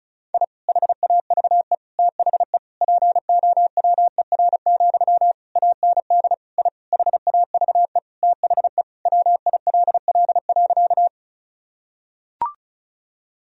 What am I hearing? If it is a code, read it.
Morse code, heard as a keyed tone, about 36 wpm, 700 Hz. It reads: I HAVE THE POWER, AND I HAVE THE WILL.
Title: Sentences from Top 1000 Words - Review Part 2 - 35wpm